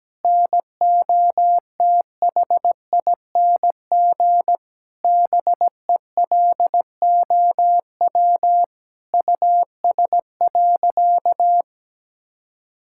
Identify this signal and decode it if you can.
Morse code, heard as a keyed tone, about 17 wpm, 700 Hz. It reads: NOTHING BELOW US.
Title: Sentences from Top 1000 Words - Review Part 5 - 17wpm